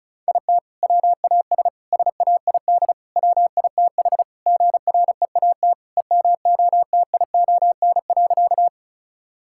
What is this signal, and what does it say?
Morse code, heard as a keyed tone, about 36 wpm, 700 Hz. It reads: IT WAS SAID WITH GREAT EMOTION.
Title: Sentences from Top 1600 Words 35wpm